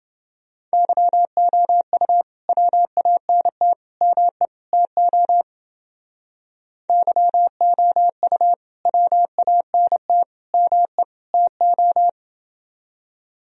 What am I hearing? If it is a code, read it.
Morse code, heard as a keyed tone, about 29 wpm, 700 Hz. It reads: YOU WANT ME TO YOU WANT ME TO